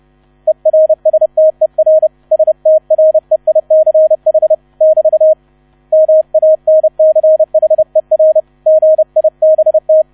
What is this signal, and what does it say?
Morse code, heard as a keyed tone, about 30 wpm, 630 Hz, with radio noise. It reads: ERSTER STREICH <BT> MANCHER GIBT